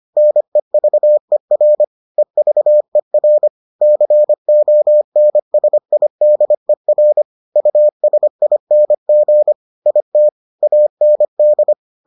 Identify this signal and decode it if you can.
Morse code, heard as a keyed tone, about 25 wpm, 600 Hz. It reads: NEVER EVER CONSIDER USING IT AND